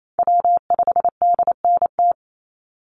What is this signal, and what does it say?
Morse code, heard as a keyed tone, about 28 wpm, 700 Hz. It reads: W5DNT
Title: Call Signs Repeated 3x - 28wpm